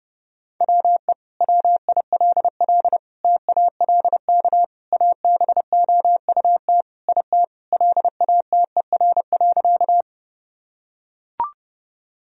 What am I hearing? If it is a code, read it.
Morse code, heard as a keyed tone, about 30 wpm, 700 Hz. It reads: WE WILL TALK ABOUT IT LATER.
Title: Sentences from Top 1000 Words - Review Part 4 - 30wpm